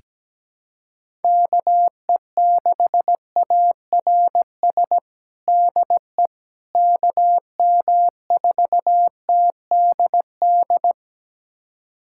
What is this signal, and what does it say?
Morse code, heard as a keyed tone, about 17 wpm, 700 Hz. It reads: KE6ARS DE KM4TDD
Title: USA Callsign DE Callsign - 6 Characters - 17wpm